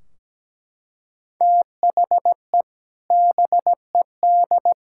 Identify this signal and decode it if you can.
Morse code, heard as a keyed tone, about 17 wpm, 700 Hz. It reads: THE BED
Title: Sentences from Top 1600 Words 17wpm